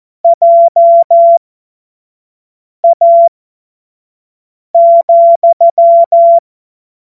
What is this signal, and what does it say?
Morse code, heard as a keyed tone, about 14 wpm, spaced as for 5 wpm, 680 Hz. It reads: JA,
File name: Morsekurs / Witze / Witz 13. Januar 2026 25 F70